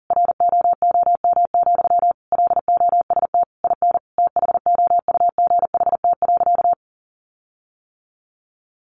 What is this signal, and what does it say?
Morse code, heard as a keyed tone, about 40 wpm, 700 Hz. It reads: ROOM, LOST IN THOUGHT.
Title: Sentences from Top 1400 Words 40wpm